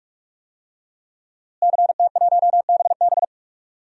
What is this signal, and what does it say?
Morse code, heard as a keyed tone, about 46 wpm, 700 Hz. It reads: CT1BB
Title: DXCC-100 Call Signs - 5 Characters - Repeated 2x - 45wpm